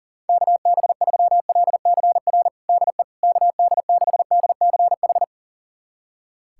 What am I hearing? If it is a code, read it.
Morse code, heard as a keyed tone, about 40 wpm, 690 Hz. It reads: KB3LCR DE KD6DCH
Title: USA Callsign DE Callsign - 6 Characters - 40wpm